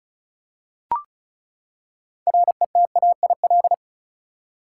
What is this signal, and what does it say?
Morse code, heard as a keyed tone, about 36 wpm, 700 Hz. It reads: RETAIL
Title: Practice QSO - Exchange 3 - More Info - 35wpm